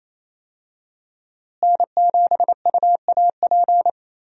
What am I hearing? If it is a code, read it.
Morse code, heard as a keyed tone, about 28 wpm, 700 Hz. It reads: N7UAP
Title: USA Call Signs - 5 Characters - Repeated 3x - 28wpm